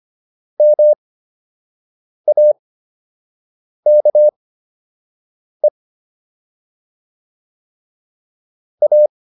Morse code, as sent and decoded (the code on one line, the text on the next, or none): -- .- -.- . / .-
MAKE A